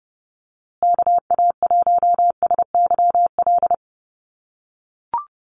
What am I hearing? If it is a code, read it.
Morse code, heard as a keyed tone, about 30 wpm, 700 Hz. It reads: KA1SYL
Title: USA Call Signs Repeated 3x 30wpm